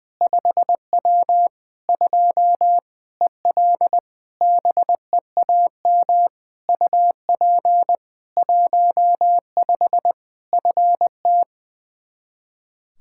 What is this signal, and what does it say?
Morse code, heard as a keyed tone, about 20 wpm, 700 Hz. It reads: 5W 2 EL BEAM UP 15 FT
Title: K1USN SST QSO Practice 20wpm